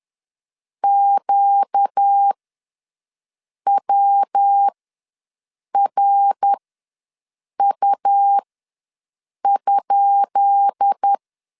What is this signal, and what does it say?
Morse code, heard as a keyed tone, about 10 wpm, spaced as for 6 wpm, 800 Hz. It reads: QWRU?
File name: LEZIONI / 10 SIGNS / 10B INT ws 10wpm 4134